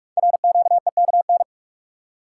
Morse code, heard as a keyed tone, about 44 wpm, 700 Hz. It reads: RQEKN
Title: Code Groups - 5 Letters - 45wpm